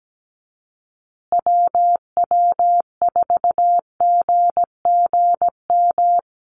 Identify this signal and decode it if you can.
Morse code, heard as a keyed tone, about 17 wpm, 700 Hz. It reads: WW4GGM
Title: USA Call Signs Repeated 2x 17wpm